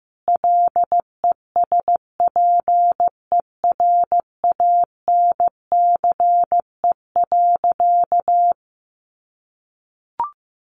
Morse code, heard as a keyed tone, about 15 wpm, 700 Hz. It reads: LESPERANCE.
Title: A Sound Of Thunder 15wpm